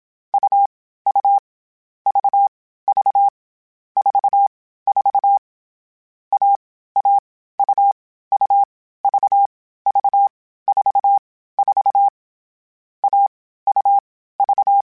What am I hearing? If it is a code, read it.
Morse code, heard as a keyed tone, about 27 wpm, 800 Hz. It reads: U U V V 4 4 A A U U V V 4 4 A U V
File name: CW Code Groups-mp3s / Sets mp3s / AUV AUV4 sp @25